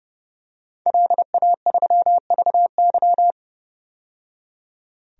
Morse code, heard as a keyed tone, about 30 wpm, 700 Hz. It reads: LA3VY
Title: DXCC-100 Call Signs - 5 Characters - 30wpm